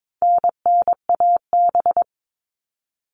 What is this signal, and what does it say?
Morse code, heard as a keyed tone, about 22 wpm, 700 Hz. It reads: NNAB
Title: All Cut Numbers - 4-Digit 22wpm